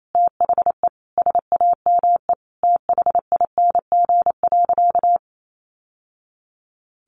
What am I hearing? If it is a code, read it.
Morse code, heard as a keyed tone, about 28 wpm, 700 Hz. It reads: THE SAME THING.
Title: Sentences from Top 800 Words 28wpm